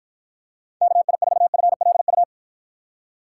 Morse code, heard as a keyed tone, about 50 wpm, 700 Hz. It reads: KI4FLU